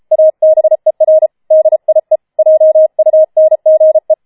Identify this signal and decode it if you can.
Morse code, heard as a keyed tone, about 33 wpm, 630 Hz. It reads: ABER DIE JUNGE
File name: Schneewittchen Teil5 34wpm